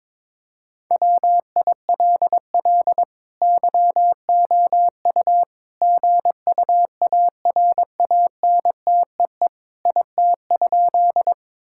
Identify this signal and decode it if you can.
Morse code, heard as a keyed tone, about 22 wpm, 700 Hz. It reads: WILL YOU GUARANTEE IT?
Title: Sentences from Top 1600 Words 22wpm